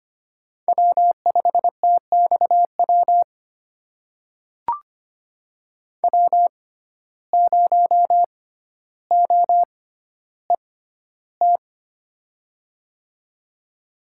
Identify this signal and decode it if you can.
Morse code, heard as a keyed tone, about 25 wpm, spaced as for 13 wpm, 700 Hz. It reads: W5TXW W0OET
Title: USA Call Signs - 5 Characters - 6x Character Spacing - 25wpm